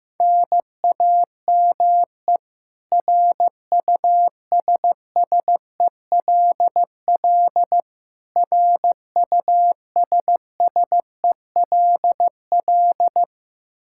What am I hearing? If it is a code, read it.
Morse code, heard as a keyed tone, about 15 wpm, 700 Hz. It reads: NAME RUSSELL RUSSELL
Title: K1USN SST QSO Practice 15wpm